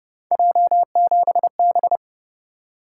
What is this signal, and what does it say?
Morse code, heard as a keyed tone, about 29 wpm, 700 Hz. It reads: J7B